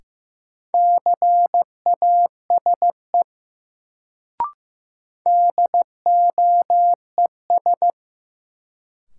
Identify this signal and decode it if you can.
Morse code, heard as a keyed tone, about 15 wpm, 700 Hz. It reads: CASE DOES
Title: Sets of 2 Words - New Word Drill - 15wpm